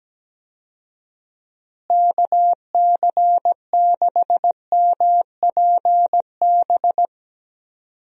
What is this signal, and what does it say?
Morse code, heard as a keyed tone, about 17 wpm, 700 Hz. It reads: KC6MPB
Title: USA Call Signs - 6 Characters - Repeated 3x - 17wpm